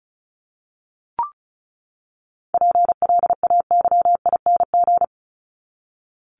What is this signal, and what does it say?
Morse code, heard as a keyed tone, about 34 wpm, 700 Hz. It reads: PLAYING
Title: ING Suffix - Top 100 Words with Base Words 35wpm